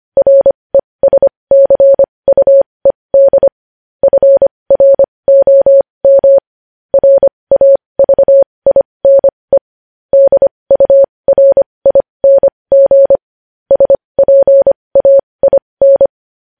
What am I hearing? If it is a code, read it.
Morse code, heard as a keyed tone, about 25 wpm, spaced as for 20 wpm, 550 Hz. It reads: RESCUED FROM RAVINE DURING SPAIN
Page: news-headlines-in-morse-code-at-20-wpm